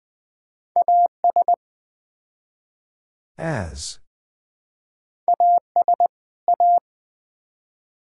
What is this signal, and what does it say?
Morse code, heard as a keyed tone, about 20 wpm, 700 Hz. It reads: AS AS A
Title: Build A Phrase - 4-Word Phrases - 20wpm